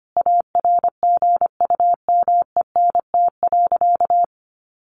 Morse code, heard as a keyed tone, about 25 wpm, 700 Hz. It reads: ARGUMENT.
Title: Sentences from Top 1400 Words 25wpm